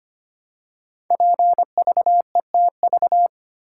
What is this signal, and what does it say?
Morse code, heard as a keyed tone, about 26 wpm, 700 Hz. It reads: PVETV
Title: Code Groups - 5 Letters - 25wpm